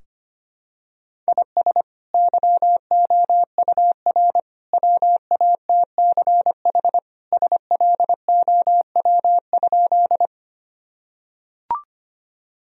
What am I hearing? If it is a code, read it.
Morse code, heard as a keyed tone, about 25 wpm, 700 Hz. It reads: IS YOUR WATCH SLOW?
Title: Sentences from Top 1000 Words - Review Part 2 - 25wpm